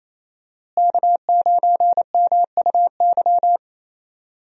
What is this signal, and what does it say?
Morse code, heard as a keyed tone, about 28 wpm, 700 Hz. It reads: K9MUY
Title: USA Call Signs - 5 Characters - Repeated 3x - 28wpm